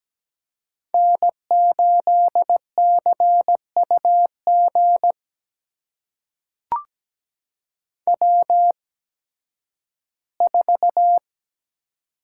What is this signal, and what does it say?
Morse code, heard as a keyed tone, about 17 wpm, spaced as for 11 wpm, 700 Hz. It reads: N8CUG W4
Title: USA Call Signs - 5 Characters - 8x Character Spacing - 17wpm